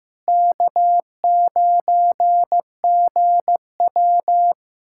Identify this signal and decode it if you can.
Morse code, heard as a keyed tone, about 15 wpm, 700 Hz. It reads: K9GW
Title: USA Callsign DE Callsign - 4 Characters - 15wpm